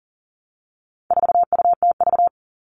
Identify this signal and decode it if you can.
Morse code, heard as a keyed tone, about 37 wpm, 700 Hz. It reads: 4UTV
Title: All Cut Numbers - 4-Digit 40wpm